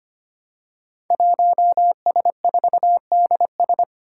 Morse code, heard as a keyed tone, about 25 wpm, 700 Hz. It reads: 1S4DS
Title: Five Challenging Letter-Number Combinations 25wpm